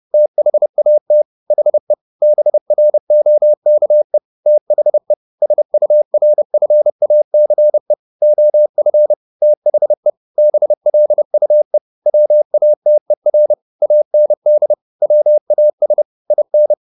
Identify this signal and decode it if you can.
Morse code, heard as a keyed tone, about 30 wpm, 600 Hz. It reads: THAT HE BROKE THE SURFACE OF THE BLUE WATER AND WAS IN